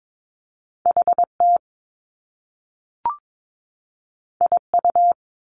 Morse code, heard as a keyed tone, about 22 wpm, 700 Hz. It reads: HT IU
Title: Top Two Letter Combinations in Words 22wpm